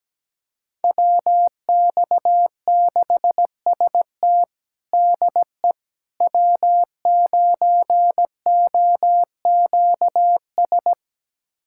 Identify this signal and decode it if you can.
Morse code, heard as a keyed tone, about 17 wpm, 700 Hz. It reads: WX6ST DE W9OQS
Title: USA Callsign DE Callsign - 5 Characters - 17wpm